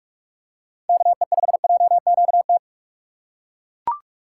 Morse code, heard as a keyed tone, about 41 wpm, 700 Hz. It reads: KE5JQT
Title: USA Call Signs - 6 Characters - 4x Character Spacing - 45wpm